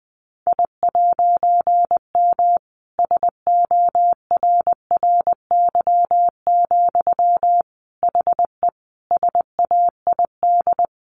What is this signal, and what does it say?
Morse code, heard as a keyed tone, about 20 wpm, 700 Hz. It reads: I'M SORRY, HE SAID
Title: A Sound Of Thunder 20wpm